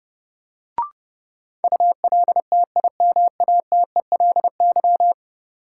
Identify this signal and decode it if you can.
Morse code, heard as a keyed tone, about 30 wpm, 700 Hz. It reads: ULTIMATELY
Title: Sentences from Top 1200 Words - New Word Drill Alt - 30wpm